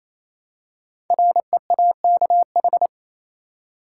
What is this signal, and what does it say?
Morse code, heard as a keyed tone, about 28 wpm, 700 Hz. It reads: REAKH
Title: Code Groups - 5 Letter:Numbers - 28wpm